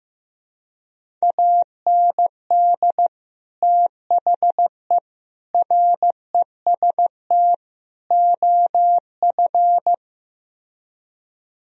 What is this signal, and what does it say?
Morse code, heard as a keyed tone, about 15 wpm, 700 Hz. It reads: AND THE REST OF
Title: Speed-Racing - Sets of 4 Words 15wpm